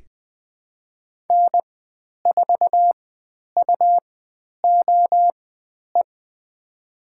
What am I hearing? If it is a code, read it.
Morse code, heard as a keyed tone, about 20 wpm, spaced as for 10 wpm, 700 Hz. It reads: N4UOE
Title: 102 Lesson 17g - Letter u - Call Signs 20wpm